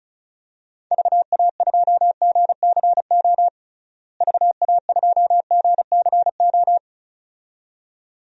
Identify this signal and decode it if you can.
Morse code, heard as a keyed tone, about 36 wpm, 700 Hz. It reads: VA2GCO VA2GCO
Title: DXCC-100 Call Signs - 6 Characters - Repeated 2x - 35wpm